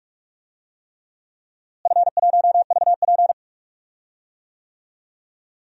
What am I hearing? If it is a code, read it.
Morse code, heard as a keyed tone, about 42 wpm, 700 Hz. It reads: F1VP
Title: DXCC-100 Call Signs - 4 Characters - Repeated 3x - 45wpm